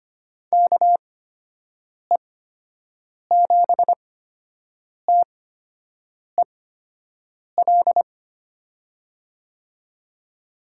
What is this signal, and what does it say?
Morse code, heard as a keyed tone, about 25 wpm, spaced as for 7 wpm, 700 Hz. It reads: KE7TEL
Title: USA Call Signs - 6 Characters - 8x Character Spacing - 25wpm